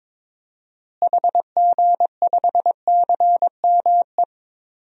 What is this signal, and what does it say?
Morse code, heard as a keyed tone, about 22 wpm, 700 Hz. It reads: HG5CME
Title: DXCC-100 Call Signs - 6 Characters - Repeated 3x - 22wpm